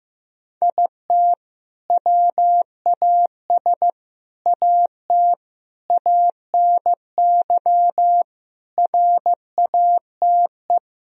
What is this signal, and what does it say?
Morse code, heard as a keyed tone, about 15 wpm, 700 Hz. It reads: IT WAS AT ANY RATE
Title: Sentences from Top 1000 Words - Review Part 4 - 15wpm